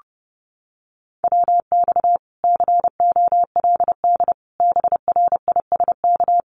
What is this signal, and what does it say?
Morse code, heard as a keyed tone, about 30 wpm, 700 Hz. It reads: WX COLD BRISK
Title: Common QSO Phrases Encore 30wpm